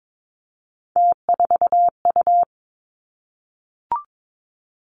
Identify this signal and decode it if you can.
Morse code, heard as a keyed tone, about 22 wpm, 700 Hz. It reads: T4U
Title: Three Challenging Letter-Number Combinations 22wpm